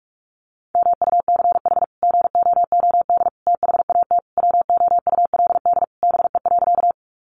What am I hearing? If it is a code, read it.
Morse code, heard as a keyed tone, about 44 wpm, 700 Hz. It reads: MUCH GOOD THAT WOULD BE.
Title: Sentences from Top 200 Words 45wpm